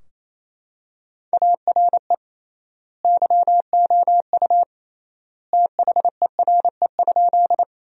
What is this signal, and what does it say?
Morse code, heard as a keyed tone, about 28 wpm, 700 Hz. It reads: ARE YOU THERE?
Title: 3x Word-Spacing - Sentences from Top 100 Words - 28wpm